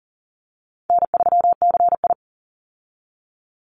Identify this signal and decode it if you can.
Morse code, heard as a keyed tone, about 40 wpm, 700 Hz. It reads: N3CI
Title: Call Signs - 40wpm